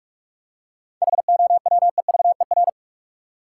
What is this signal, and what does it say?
Morse code, heard as a keyed tone, about 44 wpm, 700 Hz. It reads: HOWEVER
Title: Seven Letter Words - 45wpm